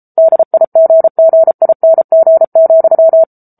Morse code, heard as a keyed tone, about 34 wpm, 650 Hz. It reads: DIGGING,
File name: Book Two / 07-Chapter 7 - 33 WPM